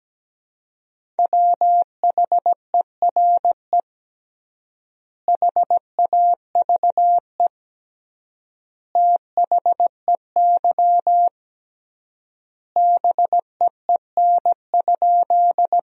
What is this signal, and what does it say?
Morse code, heard as a keyed tone, about 17 wpm, 700 Hz. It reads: WHERE HAVE THEY BEEN?
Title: 3x Word-Spacing - Sentences from Top 200 Words - 17wpm